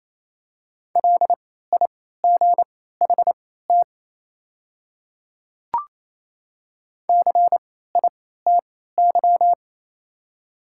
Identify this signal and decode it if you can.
Morse code, heard as a keyed tone, about 27 wpm, spaced as for 16 wpm, 700 Hz. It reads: LIGHT CITY
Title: 3x Character Spacing - Top 300 Words - 28wpm